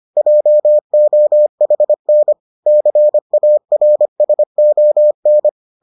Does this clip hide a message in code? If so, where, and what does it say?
Yes: Morse code from 0.2 to 5.5 s, heard as a keyed tone, about 25 wpm, 600 Hz. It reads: JOHN CARSON